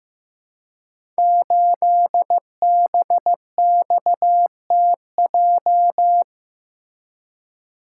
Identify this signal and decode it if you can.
Morse code, heard as a keyed tone, about 15 wpm, 700 Hz. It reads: 8BXTJ
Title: Code Groups - 5 Letter:Numbers - 15wpm